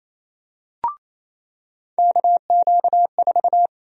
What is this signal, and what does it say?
Morse code, heard as a keyed tone, about 28 wpm, 700 Hz. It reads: KQ4
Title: USA Call Sign Prefixes - 28wpm